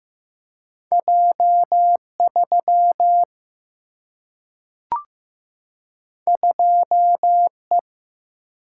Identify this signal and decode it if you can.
Morse code, heard as a keyed tone, about 15 wpm, 700 Hz. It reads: J3 2E
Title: Single-Letter Single-Number 15wpm